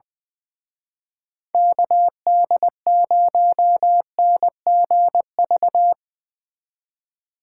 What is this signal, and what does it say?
Morse code, heard as a keyed tone, about 22 wpm, 700 Hz. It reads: KD0NGV
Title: Speed Racing - USA 2x3 Call Signs 20wpm